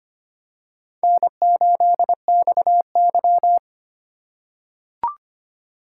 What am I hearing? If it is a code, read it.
Morse code, heard as a keyed tone, about 25 wpm, 700 Hz. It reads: N8XY
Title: USA Call Signs - 4x Character Spacing - 25wpm